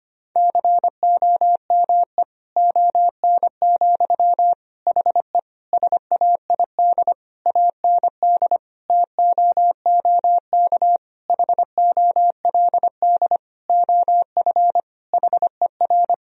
Morse code, heard as a keyed tone, about 25 wpm, 700 Hz. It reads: COME ON, HE SAID AND TOOK HOLD OF HER
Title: Sentences from Top 1000 Words - Review Part 2 - 25wpm